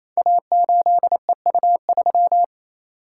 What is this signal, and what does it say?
Morse code, heard as a keyed tone, about 27 wpm, 700 Hz. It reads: A8EU3
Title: Code Groups - 5 Letter:Numbers - 28wpm